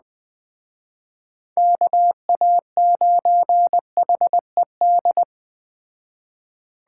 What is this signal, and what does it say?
Morse code, heard as a keyed tone, about 21 wpm, 700 Hz. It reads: KA9HED